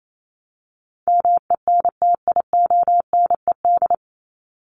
Morse code, heard as a keyed tone, about 28 wpm, 700 Hz. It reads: MENTIONED
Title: ED Suffix - Top 100 Words with Base Words 28wpm